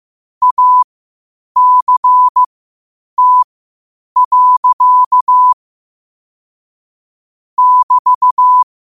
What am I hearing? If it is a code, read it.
Morse code, heard as a keyed tone, about 15 wpm, spaced as for 8 wpm, 1000 Hz. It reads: ACT. <BT>
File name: qotdfiles / 8wpm